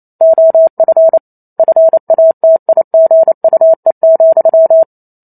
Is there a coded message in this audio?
Yes: Morse code from 0.2 to 4.8 s, heard as a keyed tone, about 28 wpm, 650 Hz. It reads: OF FATIGUE,